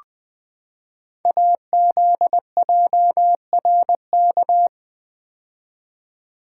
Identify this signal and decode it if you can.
Morse code, heard as a keyed tone, about 20 wpm, 700 Hz. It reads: AZJRK
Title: Code Groups - 5 Letter:Numbers - 20wpm